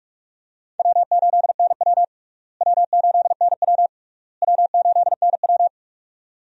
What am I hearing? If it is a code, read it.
Morse code, heard as a keyed tone, about 46 wpm, 700 Hz. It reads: W8NW W8NW W8NW
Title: USA Call Signs - 4 Characters - Repeated 3x - 45wpm